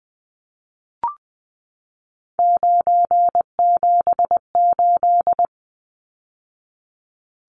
Morse code, heard as a keyed tone, about 20 wpm, 700 Hz. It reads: 978
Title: Numbers - 3-Digit 20wpm